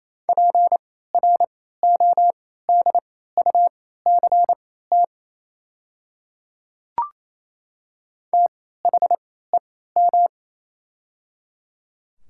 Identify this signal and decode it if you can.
Morse code, heard as a keyed tone, about 27 wpm, spaced as for 16 wpm, 700 Hz. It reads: PRODUCT THEM
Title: 3x Character Spacing - Top 500 Words - 28wpm